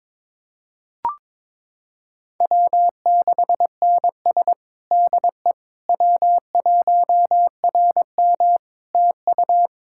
Morse code, heard as a keyed tone, about 22 wpm, 700 Hz. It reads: W6NS DE W1RM TU
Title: K1USN SST QSO Practice 22wpm